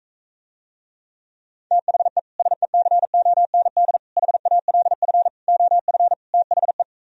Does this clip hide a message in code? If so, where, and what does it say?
Yes: Morse code from 1.7 to 6.8 s, heard as a keyed tone, about 45 wpm, 700 Hz. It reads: THE SECOND HALF OF THE